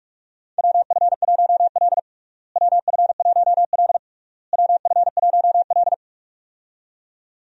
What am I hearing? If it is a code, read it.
Morse code, heard as a keyed tone, about 45 wpm, 700 Hz. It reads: WF1L WF1L WF1L
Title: USA Call Signs - 4 Characters - Repeated 3x - 45wpm